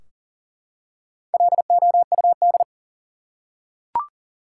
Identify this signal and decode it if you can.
Morse code, heard as a keyed tone, about 37 wpm, 700 Hz. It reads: LOUD
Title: Four Letter Words - 40wpm